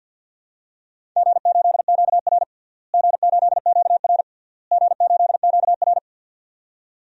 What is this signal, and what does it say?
Morse code, heard as a keyed tone, about 48 wpm, 700 Hz. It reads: G8QR G8QR G8QR
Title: DXCC-100 Call Signs - 4 Characters - Repeated 3x - 50wpm